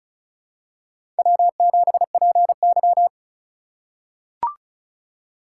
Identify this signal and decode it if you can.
Morse code, heard as a keyed tone, about 36 wpm, 700 Hz. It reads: W7PY
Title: USA Call Signs - 4 Characters - 4x Character Spacing - 35wpm